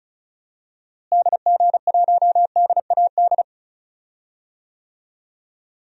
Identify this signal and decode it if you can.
Morse code, heard as a keyed tone, about 36 wpm, 700 Hz. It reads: DG1DAD
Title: DXCC-100 Call Signs - 6 Characters - Repeated 2x - 35wpm